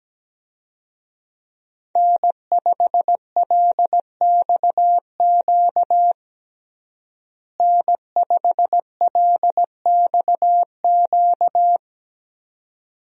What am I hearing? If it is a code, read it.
Morse code, heard as a keyed tone, about 17 wpm, 700 Hz. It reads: N5LXQ N5LXQ
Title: USA Call Signs - 5 Characters - Repeated 2x - 17wpm